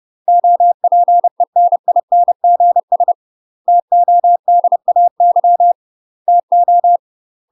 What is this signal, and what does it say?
Morse code, heard as a keyed tone, about 30 wpm, 700 Hz. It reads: OPENINGS TODAY TO
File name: UK MASTER QSOs TRAINING / 30WPM / UK MASTER QSOs 0008